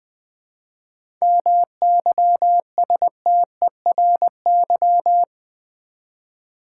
Morse code, heard as a keyed tone, about 20 wpm, 690 Hz. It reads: MYSTERY
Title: Difficult Words with Hints - 20wpm